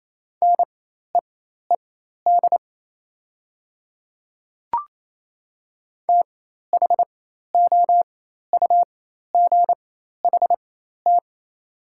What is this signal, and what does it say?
Morse code, heard as a keyed tone, about 28 wpm, spaced as for 13 wpm, 690 Hz. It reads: NEED THOUGHT
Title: 4x Character Spacing - Top 300 Words - 28wpm